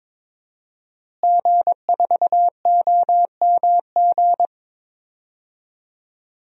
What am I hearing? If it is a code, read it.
Morse code, heard as a keyed tone, about 22 wpm, 700 Hz. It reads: G4OMG
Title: DXCC-100 Call Signs - 5 Characters - Repeated 3x - 22wpm